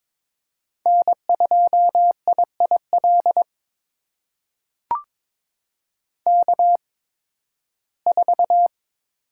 Morse code, heard as a keyed tone, about 21 wpm, spaced as for 14 wpm, 700 Hz. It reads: N2IIL K4
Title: USA Call Signs - 5 Characters - 8x Character Spacing - 22wpm